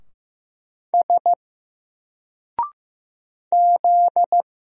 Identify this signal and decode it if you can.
Morse code, heard as a keyed tone, about 15 wpm, 700 Hz. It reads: S Z
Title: Single Letter-Number 15wpm